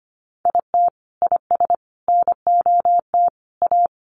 Morse code, heard as a keyed tone, about 25 wpm, 700 Hz. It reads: IT IS NOT A
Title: Sentences from Top 1500 Words 25wpm